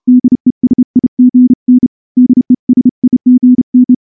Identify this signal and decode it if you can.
Morse code, heard as a keyed tone, about 29 wpm, 250 Hz. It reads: DESIGN DESIGN
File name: morse-training-30WPM